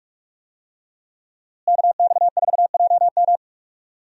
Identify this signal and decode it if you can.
Morse code, heard as a keyed tone, about 46 wpm, 700 Hz. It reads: KX4JM
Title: USA Call Signs - 5 Characters - Repeated 3x - 45wpm